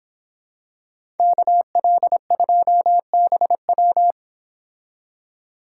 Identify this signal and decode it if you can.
Morse code, heard as a keyed tone, about 26 wpm, 700 Hz. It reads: KL2BW
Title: Speed Racing - USA 2x2 Call Signs 22wpm